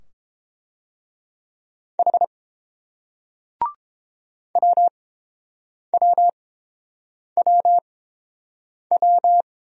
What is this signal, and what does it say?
Morse code, heard as a keyed tone, about 30 wpm, spaced as for 7 wpm, 700 Hz. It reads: H WWWW